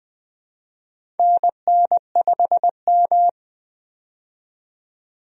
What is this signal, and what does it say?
Morse code, heard as a keyed tone, about 20 wpm, 700 Hz. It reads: NN5M
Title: USA Call Signs - 4 Characters - 20wpm